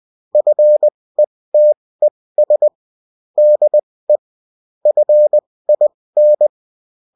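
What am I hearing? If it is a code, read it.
Morse code, heard as a keyed tone, about 20 wpm, spaced as for 16 wpm, 600 Hz. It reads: FETES DE FIN
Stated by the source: UFT bulletin sent in CW.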